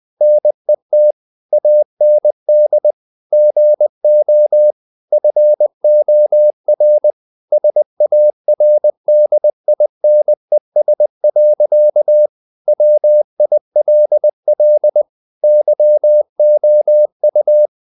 Morse code, heard as a keyed tone, about 20 wpm, 600 Hz. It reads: NET AND GO FOR SARDINES. WILL YOU
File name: the old man and the sea / the old man and the sea part 1 20wpm